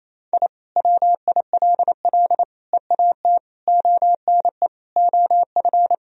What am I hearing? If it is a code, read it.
Morse code, heard as a keyed tone, about 28 wpm, 700 Hz. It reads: I WILL EAT ONE OF
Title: Sentences from Top 1000 Words - Review Part 1 - 28wpm